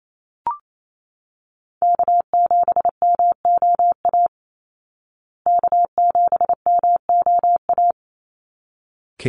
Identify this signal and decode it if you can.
Morse code, heard as a keyed tone, about 28 wpm, 700 Hz. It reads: K7MOA K7MOA
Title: Call Signs Repeated 3x - 28wpm